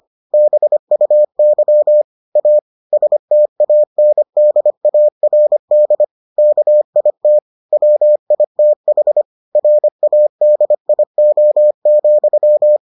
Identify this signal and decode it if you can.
Morse code, heard as a keyed tone, about 25 wpm, 600 Hz. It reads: BUY A STANDARD KIT WITH RADIO,